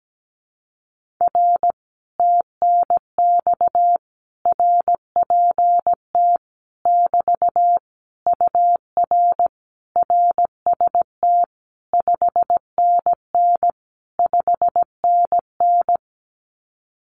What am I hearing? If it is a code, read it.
Morse code, heard as a keyed tone, about 17 wpm, 700 Hz. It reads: R TNX RPT <BT> UR RST 5NN 5NN
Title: Common QSO Phrases Encore 17wpm